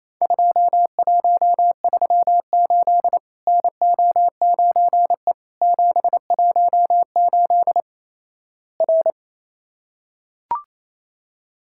Morse code, heard as a keyed tone, about 28 wpm, 700 Hz. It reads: 2138 NO9E 718 R
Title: Worked All Europe DX Contest - QTC Exchange - 28wpm